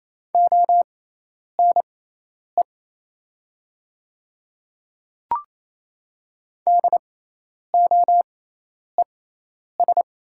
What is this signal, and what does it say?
Morse code, heard as a keyed tone, about 27 wpm, spaced as for 10 wpm, 690 Hz. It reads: ONE DOES
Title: 6x Character Spacing - Top 300 Words - 28wpm